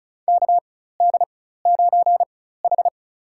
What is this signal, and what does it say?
Morse code, heard as a keyed tone, about 36 wpm, spaced as for 16 wpm, 700 Hz. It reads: KD9H
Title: USA Call Signs - 4 Characters - 4x Character Spacing - 35wpm